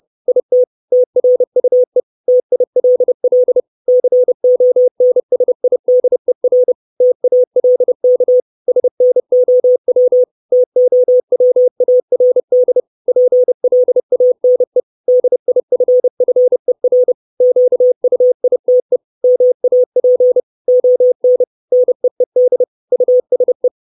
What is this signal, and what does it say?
Morse code, heard as a keyed tone, about 30 wpm, 500 Hz. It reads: IT TRUE TILL CONSIDER TALK SNOW TOWARD PLANE DIFFER QUITE MAP ON NEED USE
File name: 632 Words 30WPM